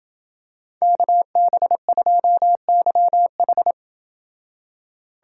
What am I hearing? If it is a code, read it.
Morse code, heard as a keyed tone, about 28 wpm, 700 Hz. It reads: KB2YH